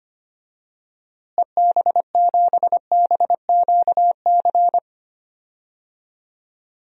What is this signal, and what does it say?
Morse code, heard as a keyed tone, about 25 wpm, 700 Hz. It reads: EB7BQC
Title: DXCC-100 Call Signs - 6 Characters - Repeated 2x - 25wpm